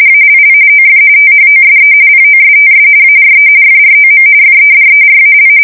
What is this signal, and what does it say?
RTTY, about 45.45 baud. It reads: ALL OF THOSE POSTS ARE FROM FEDERAL
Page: 170 Shift 2125/2295